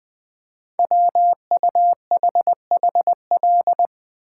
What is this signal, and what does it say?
Morse code, heard as a keyed tone, about 20 wpm, 700 Hz. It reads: WUHHL
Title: Code Groups - 5 Letters - 20wpm